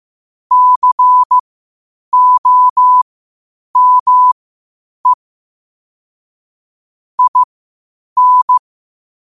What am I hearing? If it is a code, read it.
Morse code, heard as a keyed tone, about 15 wpm, spaced as for 8 wpm, 1000 Hz. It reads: COME IN
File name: qotdfiles / 8wpm